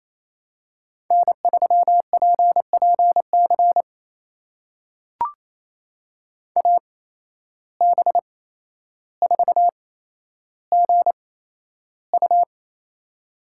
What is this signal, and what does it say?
Morse code, heard as a keyed tone, about 28 wpm, spaced as for 12 wpm, 700 Hz. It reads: N3PPC AB4GU
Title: USA Call Signs - 5 Characters - 8x Character Spacing - 28wpm